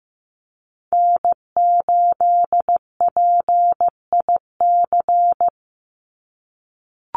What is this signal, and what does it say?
Morse code, heard as a keyed tone, about 15 wpm, 700 Hz. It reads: N8PIC
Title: USA Call Signs 15wpm